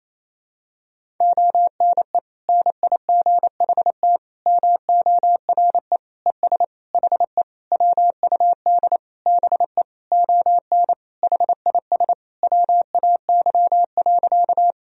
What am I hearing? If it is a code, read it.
Morse code, heard as a keyed tone, about 28 wpm, 700 Hz. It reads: ONE NIGHT MORE ES HE WUD BE ON HIS WAY.
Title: Sentences from Top 300 Words and Abbreviations 28wpm